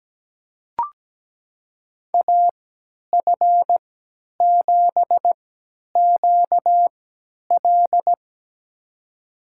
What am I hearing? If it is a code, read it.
Morse code, heard as a keyed tone, about 17 wpm, spaced as for 10 wpm, 700 Hz. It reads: AF7QL
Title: USA Call Signs - 5 Characters - 3x Character Spacing - 17wpm